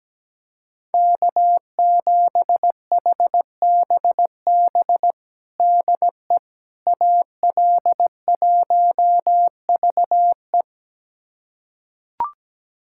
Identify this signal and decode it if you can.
Morse code, heard as a keyed tone, about 17 wpm, 700 Hz. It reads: K7HBB DE AL1VE
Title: USA Callsign DE Callsign - 5 Characters - 17wpm